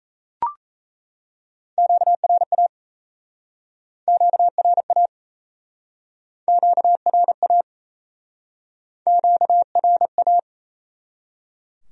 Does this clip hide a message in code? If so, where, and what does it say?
Yes: Morse code from 1.8 to 10.4 s, heard as a keyed tone, about 37 wpm, 700 Hz. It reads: QRA QRA QRA QRA